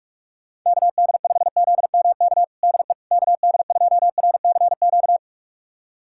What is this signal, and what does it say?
Morse code, heard as a keyed tone, about 46 wpm, 700 Hz. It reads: KD5ZMK DE KD2RCQ
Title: USA Callsign DE Callsign - 6 Characters - 45wpm